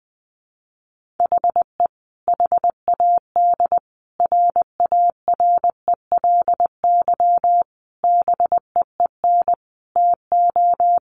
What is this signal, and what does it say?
Morse code, heard as a keyed tone, about 20 wpm, 700 Hz. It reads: HE HAD RARELY BEEN TO
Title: Sentences from Top 1400 Words 20wpm